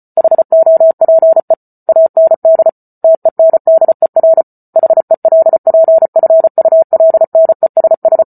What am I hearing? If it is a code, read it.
Morse code, heard as a keyed tone, about 34 wpm, 650 Hz. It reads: HOPE AND TENDER HELPFULNESS
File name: Book Two / 08-Chapter 8 - 34 WPM